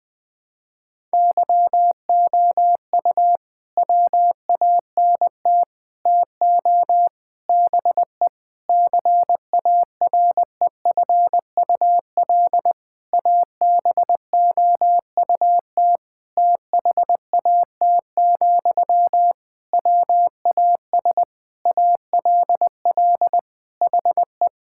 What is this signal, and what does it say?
Morse code, heard as a keyed tone, about 20 wpm, 700 Hz. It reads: YOU WANT TO BE CAREFUL ABOUT THAT, WAS ALL HE
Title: Sentences from Top 1600 Words 20wpm